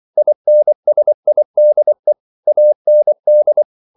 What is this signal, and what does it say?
Morse code, heard as a keyed tone, about 24 wpm, 600 Hz. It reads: INSIDE AND